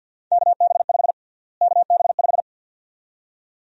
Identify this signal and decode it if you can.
Morse code, heard as a keyed tone, about 50 wpm, 700 Hz. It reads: KB5 KB5
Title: USA Call Sign Prefixes - Repeated 2x - 50wpm